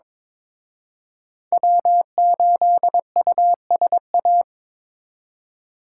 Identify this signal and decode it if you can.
Morse code, heard as a keyed tone, about 25 wpm, 700 Hz. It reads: W8USA
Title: Speed Racing - USA 1x3 Call Signs 22wpm